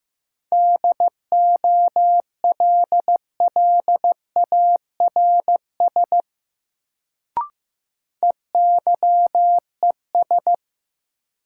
Morse code, heard as a keyed tone, about 15 wpm, 700 Hz. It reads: DOLLARS EYES
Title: Sets of 5 Words - New Word Drill - 15wpm